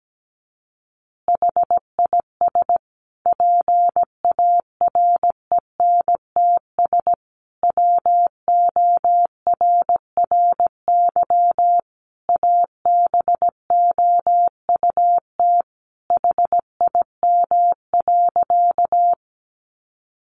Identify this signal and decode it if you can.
Morse code, heard as a keyed tone, about 17 wpm, 700 Hz. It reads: HIS PARENTS WORRY ABOUT HIM.
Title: Easy Phrases 17wpm